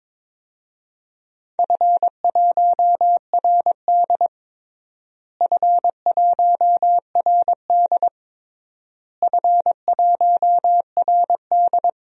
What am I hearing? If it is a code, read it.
Morse code, heard as a keyed tone, about 22 wpm, 700 Hz. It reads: F1RD F1RD F1RD
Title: DXCC-100 Call Signs - 4 Characters - Repeated 3x - 22wpm